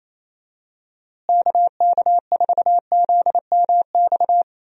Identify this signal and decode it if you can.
Morse code, heard as a keyed tone, about 27 wpm, 700 Hz. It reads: KK4ZMX
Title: USA Call Signs - 6 Characters - Repeated 2x - 28wpm